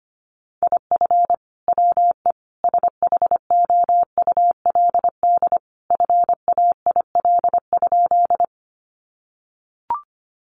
Morse code, heard as a keyed tone, about 25 wpm, 700 Hz. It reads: IF WE SHOULD FAIL?
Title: Sentences from Top 1100 Words 25wpm